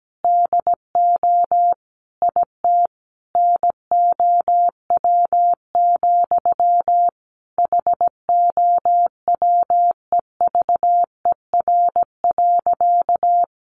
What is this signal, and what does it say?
Morse code, heard as a keyed tone, about 17 wpm, 700 Hz. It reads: DO IT NOW, HOWEVER.
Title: Sentences from Top 600 Words 17wpm